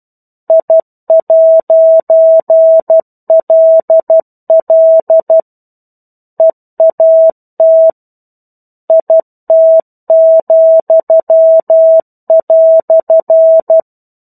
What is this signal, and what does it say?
Morse code, heard as a keyed tone, about 12 wpm, 650 Hz. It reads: I'LL EAT IT,"
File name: 01-Chapter-01 12 WPM